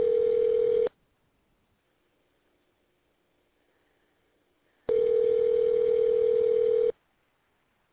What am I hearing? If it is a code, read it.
Ringback tone (North America), 440 + 480 Hz.